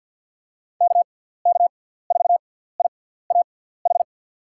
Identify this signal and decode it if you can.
Morse code, heard as a keyed tone, about 51 wpm, spaced as for 17 wpm, 700 Hz. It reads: KK4IAH
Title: USA Call Signs - 6 Characters - 6x Character Spacing - 50wpm